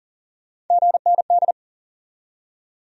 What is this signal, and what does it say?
Morse code, heard as a keyed tone, about 54 wpm, spaced as for 47 wpm, 700 Hz. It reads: GND
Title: Speed-Racing Abbreviations - 40wpm